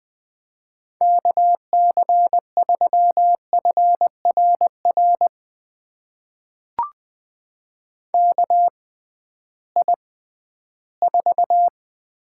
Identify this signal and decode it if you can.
Morse code, heard as a keyed tone, about 20 wpm, spaced as for 13 wpm, 700 Hz. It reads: KC3FRR KI4
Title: USA Call Signs - 6 Characters - 6x Character Spacing - 20wpm